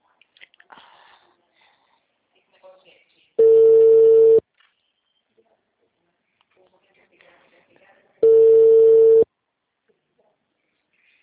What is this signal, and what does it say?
Ringback tone (China, 450 Hz), with speech.
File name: out- -2004- - - .9370